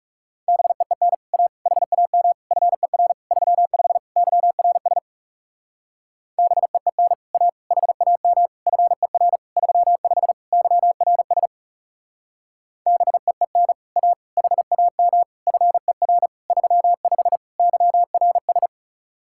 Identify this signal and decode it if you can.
Morse code, heard as a keyed tone, about 38 wpm, 700 Hz. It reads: BEEN A HAM FER 35 YRS BEEN A HAM FER 35 YRS BEEN A HAM FER 35 YRS